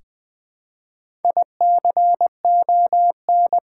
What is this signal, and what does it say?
Morse code, heard as a keyed tone, about 20 wpm, 700 Hz. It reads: ICON
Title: Build A Word - 1 Letter At A Time - 4-Letter Words - 20wpm